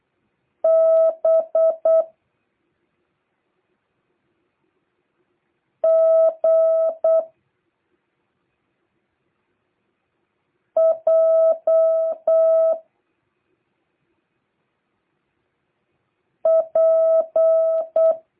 Morse code, heard as a keyed tone, about 8 wpm, 650 Hz. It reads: B G J P